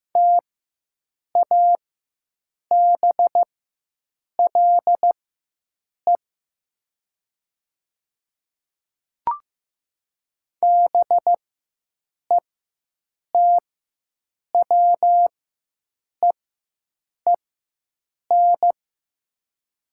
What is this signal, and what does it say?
Morse code, heard as a keyed tone, about 15 wpm, spaced as for 7 wpm, 700 Hz. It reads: TABLE BETWEEN
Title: 4x Character Spacing - Top 500 Words - 15wpm